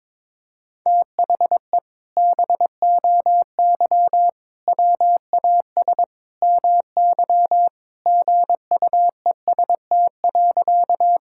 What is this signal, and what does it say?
Morse code, heard as a keyed tone, about 22 wpm, 700 Hz. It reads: THE BOY WAS MY GUEST.
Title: Sentences from Top 1600 Words 22wpm